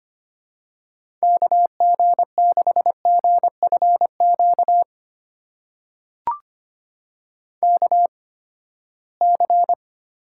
Morse code, heard as a keyed tone, about 25 wpm, spaced as for 17 wpm, 700 Hz. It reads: KG6GFQ KC
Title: USA Call Signs - 6 Characters - 8x Character Spacing - 25wpm